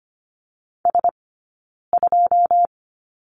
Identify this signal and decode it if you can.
Morse code, heard as a keyed tone, about 25 wpm, 700 Hz. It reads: S 2
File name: Lesson 15d - Number 5 - Two Characters 25wpm